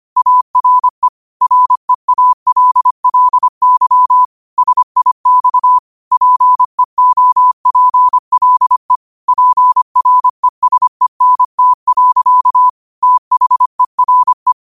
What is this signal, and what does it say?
Morse code, heard as a keyed tone, about 25 wpm, 1000 Hz. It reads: ARE REALLY SIX PEOPLE PRESENT. THERE